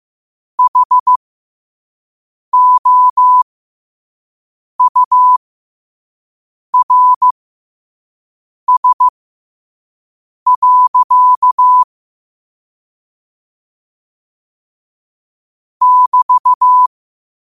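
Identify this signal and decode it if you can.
Morse code, heard as a keyed tone, about 15 wpm, spaced as for 5 wpm, 1000 Hz. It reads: HOURS. <BT>
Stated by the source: 5 WPM Morse quote of the day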